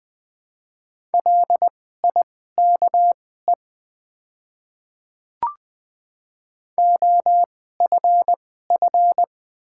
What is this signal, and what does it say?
Morse code, heard as a keyed tone, about 21 wpm, spaced as for 15 wpm, 700 Hz. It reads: LIKE OFF
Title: 2x Character Spacing - Top 300 Words - 20wpm